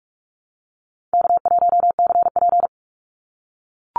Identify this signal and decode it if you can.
Morse code, heard as a keyed tone, about 44 wpm, 700 Hz. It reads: K1CP
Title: Easy Call Signs - 45wpm